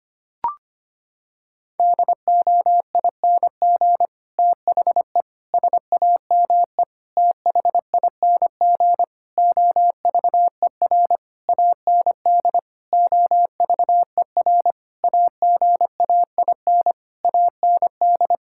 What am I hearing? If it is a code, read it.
Morse code, heard as a keyed tone, about 25 wpm, 700 Hz. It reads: DOING THE SAME THING OVER AND OVER AGAIN AND
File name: Sets of 9 Words - 25wpm